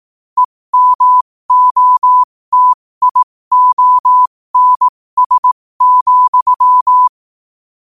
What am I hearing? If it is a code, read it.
Morse code, heard as a keyed tone, about 18 wpm, spaced as for 15 wpm, 1000 Hz. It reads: EMOTIONS,
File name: qotdfiles / 15wpm